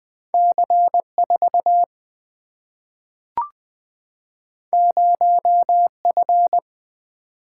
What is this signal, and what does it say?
Morse code, heard as a keyed tone, about 20 wpm, 700 Hz. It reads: C 4 0 F
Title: Single-Letter Single-Number 20wpm